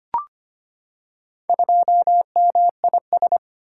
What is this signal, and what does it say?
Morse code, heard as a keyed tone, about 25 wpm, 700 Hz. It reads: 2MIS
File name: practice / 25wpm / Four Letter-Number Combinations - 25wpm